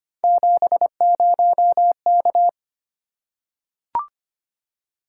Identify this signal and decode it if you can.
Morse code, heard as a keyed tone, about 25 wpm, 700 Hz. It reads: 70K
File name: Three Letter-Number Combinations - 25wpm